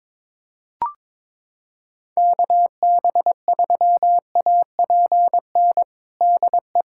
Morse code, heard as a keyed tone, about 22 wpm, 700 Hz. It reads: KB3APN DE
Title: USA Callsign DE Callsign - 6 Characters - 22wpm